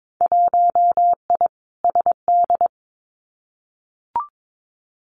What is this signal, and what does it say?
Morse code, heard as a keyed tone, about 22 wpm, 700 Hz. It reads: 1I SD
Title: Winter Field Day Exchanges 22wpm